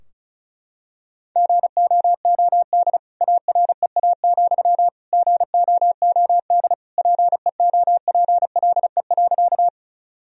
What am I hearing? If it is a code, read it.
Morse code, heard as a keyed tone, about 34 wpm, 700 Hz. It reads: GOOD AREA, GOOD PEOPLE.
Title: Sentences from Top 1000 Words - Review Part 2 - 35wpm